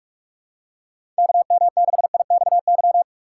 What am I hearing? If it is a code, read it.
Morse code, heard as a keyed tone, about 47 wpm, 700 Hz. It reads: KM6IXY
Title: USA Call Signs - 6 Characters - 2x Character Spacing - 45wpm